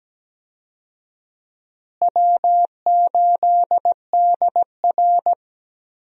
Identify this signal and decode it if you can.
Morse code, heard as a keyed tone, about 17 wpm, 700 Hz. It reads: W8DR
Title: USA Call Signs - 4 Characters - Repeated 2x - 17wpm